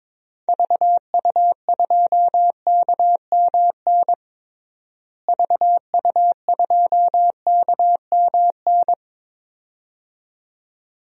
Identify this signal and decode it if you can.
Morse code, heard as a keyed tone, about 22 wpm, 700 Hz. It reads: VU2KMN VU2KMN
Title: DXCC-100 Call Signs - 6 Characters - Repeated 2x - 22wpm